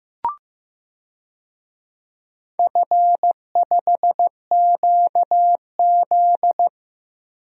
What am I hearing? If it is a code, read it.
Morse code, heard as a keyed tone, about 15 wpm, 700 Hz. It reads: F5QZ
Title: DXCC-100 Call Signs - 4 Characters - 15wpm